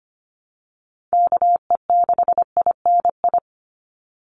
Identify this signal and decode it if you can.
Morse code, heard as a keyed tone, about 25 wpm, 700 Hz. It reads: KE6INI
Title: USA Call Signs 25wpm